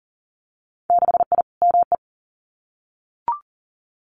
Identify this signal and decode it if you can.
Morse code, heard as a keyed tone, about 40 wpm, 700 Hz. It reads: 6I ME
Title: Winter Field Day Exchanges Encore 40wpm